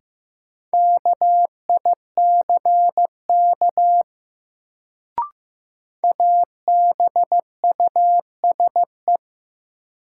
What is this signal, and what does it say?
Morse code, heard as a keyed tone, about 16 wpm, 700 Hz. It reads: KICK ABUSE
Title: Sentences from Top 1300 Words - New Word Drill - 15wpm